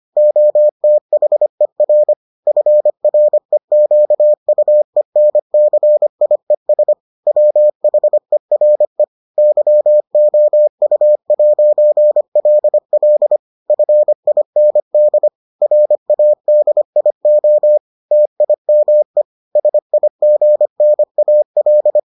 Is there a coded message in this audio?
Yes: Morse code from 0.2 to 22.0 s, heard as a keyed tone, about 25 wpm, 600 Hz. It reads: OTHER FREQUENCIES WHERE YOU'LL FIND RADIO TIME SIGNAL